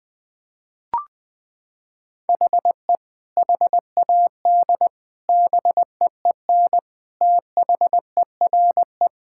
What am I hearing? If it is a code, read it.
Morse code, heard as a keyed tone, about 20 wpm, 700 Hz. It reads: HE HAD BEEN THERE
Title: Sentences with 3-Word Phrases - 20wpm